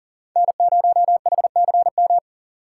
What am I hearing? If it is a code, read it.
Morse code, heard as a keyed tone, about 40 wpm, 700 Hz. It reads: N0HCM
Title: USA Call Signs - 5 Characters - 4x Character Spacing - 40wpm